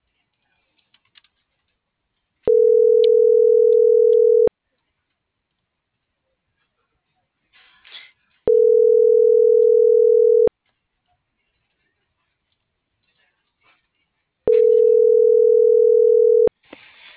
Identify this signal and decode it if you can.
Ringback tone (North America), 440 + 480 Hz.